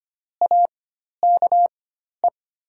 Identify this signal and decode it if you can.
Morse code, heard as a keyed tone, about 25 wpm, spaced as for 12 wpm, 700 Hz. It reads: AKE
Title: 4x Character Spacing - Top 200 Words - 25wpm